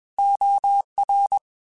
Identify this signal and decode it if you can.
Morse code, heard as a keyed tone, about 21 wpm, 800 Hz. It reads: OR
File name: Technical Subjects / Technical Subjects-mp3s / OR Gate @20